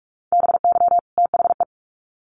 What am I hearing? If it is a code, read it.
Morse code, heard as a keyed tone, about 44 wpm, 700 Hz. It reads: BY THE
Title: Sets of 2 Words 45wpm